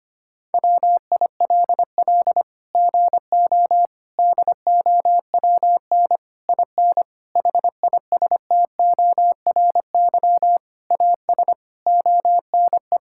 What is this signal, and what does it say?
Morse code, heard as a keyed tone, about 25 wpm, 700 Hz. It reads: WILL GO DOWN IN HISTORY AS ONE
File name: Sets of 9 Words - 25wpm